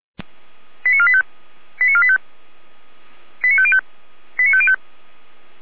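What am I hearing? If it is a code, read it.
Five-tone selective call (ZVEI), heard as quick runs of five tones, tones of about 70 ms. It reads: 89485 89485 89505 89505
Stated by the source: ZVEI five-tone call received via SDR.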